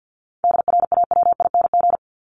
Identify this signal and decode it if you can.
Morse code, heard as a keyed tone, about 50 wpm, 700 Hz. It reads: DRAWING
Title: ING Suffix - Top 100 Words with Base Words 50wpm